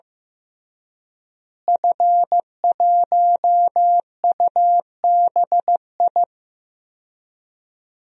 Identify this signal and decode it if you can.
Morse code, heard as a keyed tone, about 15 wpm, 700 Hz. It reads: F1UBI
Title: DXCC-100 Call Signs - 5 Characters - Repeated 3x - 15wpm